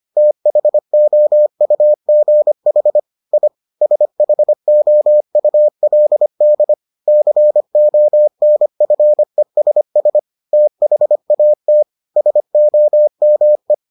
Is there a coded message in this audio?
Yes: Morse code from 0.2 to 13.7 s, heard as a keyed tone, about 25 wpm, 600 Hz. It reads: THOUGH I SHOULD CONFESS THAT SOME